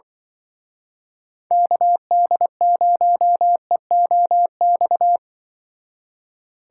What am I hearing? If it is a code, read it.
Morse code, heard as a keyed tone, about 23 wpm, 700 Hz. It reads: KD0EOX
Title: Speed Racing - USA Call Signs 20wpm